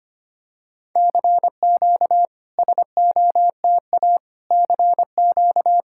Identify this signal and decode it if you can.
Morse code, heard as a keyed tone, about 25 wpm, 700 Hz. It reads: CQ SOTA CQ
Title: SOTA QSO Practice 25wpm